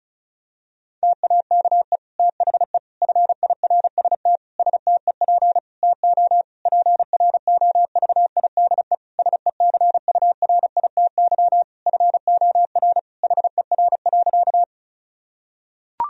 Morse code, heard as a keyed tone, about 36 wpm, 700 Hz. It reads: TAKE THE FIRST STEP TO PROVIDE SECURITY FOR HER.
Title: Sentences from Top 1000 Words - Review Part 4 - 35wpm